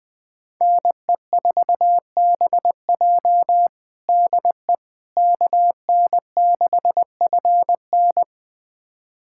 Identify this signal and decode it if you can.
Morse code, heard as a keyed tone, about 21 wpm, 700 Hz. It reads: NE4BJ DE KN6FN
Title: USA Callsign DE Callsign - 5 Characters - 20wpm